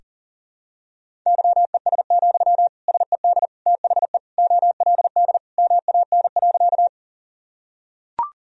Morse code, heard as a keyed tone, about 40 wpm, 700 Hz. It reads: YES, SED THE OLD MAN.
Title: Sentences from Top 500 Words and Abbreviations 40wpm